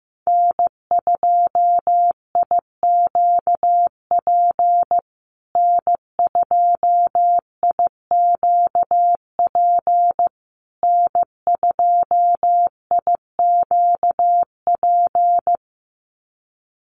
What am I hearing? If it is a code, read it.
Morse code, heard as a keyed tone, about 15 wpm, 700 Hz. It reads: N2IQP N2IQP N2IQP
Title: USA Call Signs Repeated 3x 15wpm